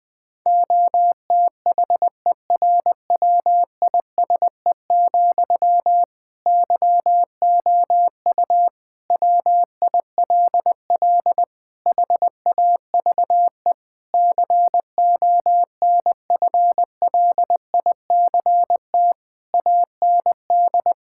Morse code, heard as a keyed tone, about 20 wpm, 700 Hz. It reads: OTHERWISE, YOU WILL HAVE CONFLICT AND
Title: Sentences from Top 1600 Words 20wpm